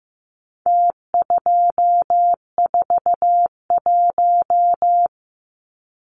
Morse code, heard as a keyed tone, about 15 wpm, 700 Hz. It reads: T241
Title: Numbers - Contest Style - 4-Digit 15wpm